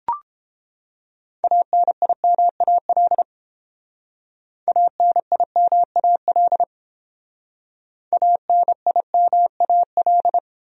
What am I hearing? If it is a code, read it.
Morse code, heard as a keyed tone, about 29 wpm, 700 Hz. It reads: ANIMAL ANIMAL ANIMAL